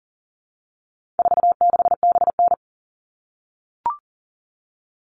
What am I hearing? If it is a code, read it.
Morse code, heard as a keyed tone, about 40 wpm, 700 Hz. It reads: 46BN
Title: All Cut Numbers - 4-Digit 40wpm